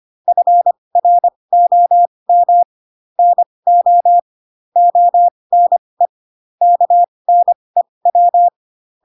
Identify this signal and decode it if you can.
Morse code, heard as a keyed tone, about 25 wpm, spaced as for 20 wpm, 700 Hz. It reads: FROM NO ONE KNEW
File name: Pinocchio / Pinocchio ch2 15 25 ik1hge